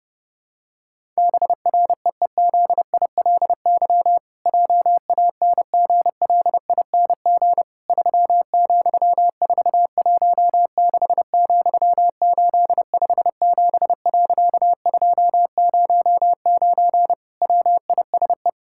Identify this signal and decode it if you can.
Morse code, heard as a keyed tone, about 30 wpm, 700 Hz. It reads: BREEZILY JANGLING 3,416,857.209 WISE
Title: Pangram Challenge v2 - 30wpm